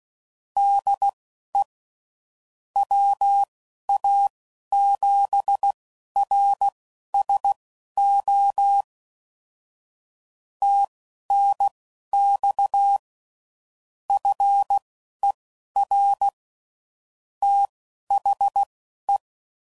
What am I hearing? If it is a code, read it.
Morse code, heard as a keyed tone, about 16 wpm, spaced as for 11 wpm, 800 Hz. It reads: DE WA7RSO TNX FER THE
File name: CW Code Groups @10WPM / QSOs / QSO kc7sac